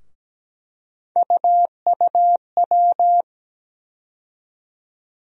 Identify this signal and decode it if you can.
Morse code, heard as a keyed tone, about 17 wpm, 700 Hz. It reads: UUW
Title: USA Call Sign Postfixes - 3 Letters 17wpm